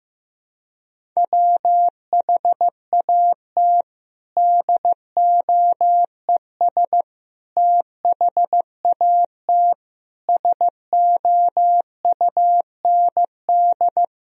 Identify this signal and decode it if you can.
Morse code, heard as a keyed tone, about 16 wpm, 700 Hz. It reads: WHAT DOES THAT SOUND
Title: Sentences from Top 1000 Words - Review Part 1 - 15wpm